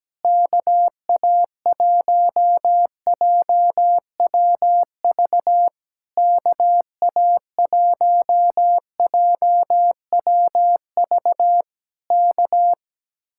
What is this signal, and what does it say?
Morse code, heard as a keyed tone, about 17 wpm, 700 Hz. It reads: KA1JWV KA1JWV K
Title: Speed-Racing Common QSO Phrases - 17wpm